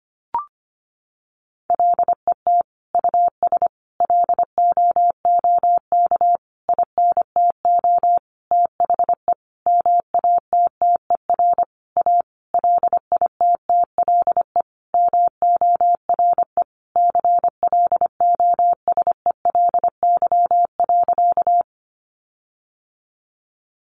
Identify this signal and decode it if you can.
Morse code, heard as a keyed tone, about 25 wpm, 700 Hz. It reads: LET US LOOK INTO THE MATTER A LITTLE MORE CLOSELY.
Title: Sentences from Top 1400 Words 25wpm